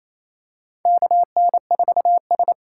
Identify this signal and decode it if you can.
Morse code, heard as a keyed tone, about 28 wpm, 700 Hz. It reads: KN4S
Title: USA Call Signs - 4 Characters - 2x Character Spacing - 28wpm